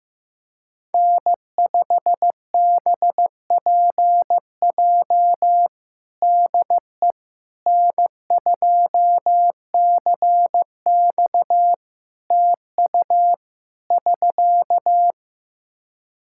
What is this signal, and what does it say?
Morse code, heard as a keyed tone, about 15 wpm, 700 Hz. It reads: N5BPJ DE N2CX TU <SK>
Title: K1USN SST QSO Practice 15wpm